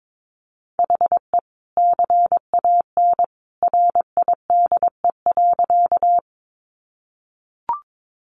Morse code, heard as a keyed tone, about 22 wpm, 700 Hz. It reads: HE CAN RIDE.
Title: Sentences from Top 1200 Words 22wpm